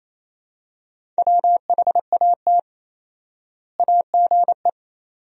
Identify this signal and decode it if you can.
Morse code, heard as a keyed tone, about 28 wpm, 700 Hz. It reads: WHAT AGE
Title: 4x Word-Spacing - Sentences from Top 500 Words - 28wpm